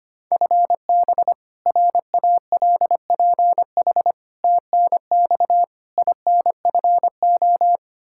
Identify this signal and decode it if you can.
Morse code, heard as a keyed tone, about 25 wpm, 700 Hz. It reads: FB RALPH TNX INFO
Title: Practice QSO - Exchange 3 - More Info - 25wpm